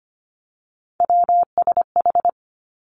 Morse code, heard as a keyed tone, about 25 wpm, 700 Hz. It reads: WSH
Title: Three Challenging Letter-Number Combinations 25wpm